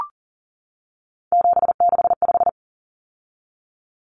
Morse code, heard as a keyed tone, about 40 wpm, 700 Hz. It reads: 765
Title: Numbers - 3-Digit 40wpm